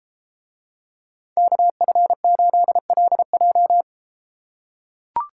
Morse code, heard as a keyed tone, about 32 wpm, 700 Hz. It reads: KF8LJ